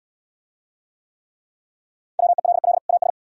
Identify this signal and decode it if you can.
Morse code, heard as a keyed tone, about 69 wpm, 700 Hz. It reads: WELL AS